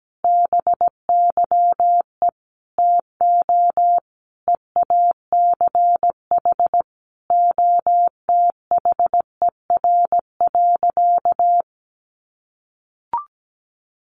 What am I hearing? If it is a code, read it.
Morse code, heard as a keyed tone, about 17 wpm, 700 Hz. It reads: BYE TO EACH OTHER.
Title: Easy Phrases 17wpm